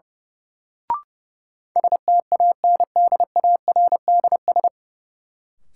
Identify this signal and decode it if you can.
Morse code, heard as a keyed tone, about 30 wpm, 700 Hz. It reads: STANDARDS
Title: Sentences from Top 1000 Words - Review Part 6 - New Word Drill - 30wpm